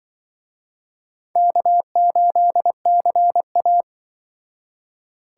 Morse code, heard as a keyed tone, about 24 wpm, 700 Hz. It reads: K8CA
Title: Speed Racing - USA 1x2 Call Signs 20wpm